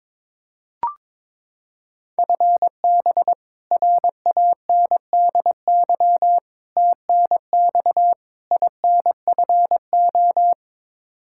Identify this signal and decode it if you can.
Morse code, heard as a keyed tone, about 22 wpm, 700 Hz. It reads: FB RANDY TNX INFO
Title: Practice QSO - Exchange 3 - More Info - 22wpm